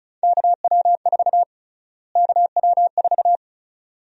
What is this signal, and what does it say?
Morse code, heard as a keyed tone, about 34 wpm, 700 Hz. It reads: KW4 KW4
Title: USA Call Sign Prefixes - Repeated 2x - 35wpm